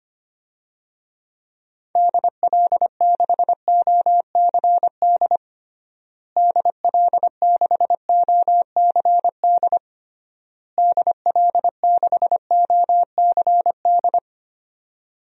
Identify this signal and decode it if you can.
Morse code, heard as a keyed tone, about 25 wpm, 700 Hz. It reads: DL6OCD DL6OCD DL6OCD
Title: DXCC-100 Call Signs - 6 Characters - Repeated 3x - 25wpm